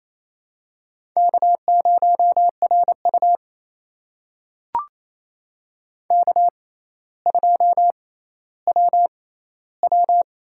Morse code, heard as a keyed tone, about 28 wpm, spaced as for 14 wpm, 700 Hz. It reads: K0RU K2WW
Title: USA Call Signs - 4 Characters - 6x Character Spacing - 28wpm